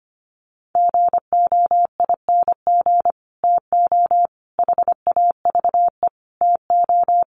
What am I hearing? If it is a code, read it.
Morse code, heard as a keyed tone, about 25 wpm, 700 Hz. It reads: GOING TO HAVE TO
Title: Sets of 5 Words Encore2 25wpm